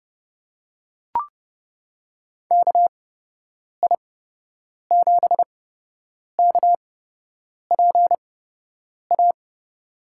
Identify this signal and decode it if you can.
Morse code, heard as a keyed tone, about 30 wpm, spaced as for 8 wpm, 700 Hz. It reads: KI7KPA
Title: USA Call Signs - 8x Character Spacing - 30wpm